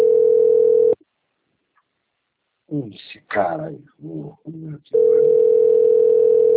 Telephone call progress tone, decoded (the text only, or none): ringback tone